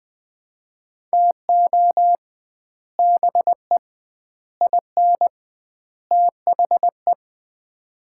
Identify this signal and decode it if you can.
Morse code, heard as a keyed tone, about 20 wpm, 700 Hz. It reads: TO BE IN THE
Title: 2x Word-Spacing - Sets of 4 Words - 20wpm